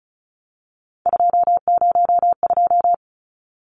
Morse code, heard as a keyed tone, about 34 wpm, 700 Hz. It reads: 202
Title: Numbers - 3-Digit 35wpm